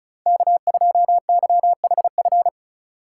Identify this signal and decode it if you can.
Morse code, heard as a keyed tone, about 36 wpm, 700 Hz. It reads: K2YHF
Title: USA Call Signs - 3x Character Spacing - 35wpm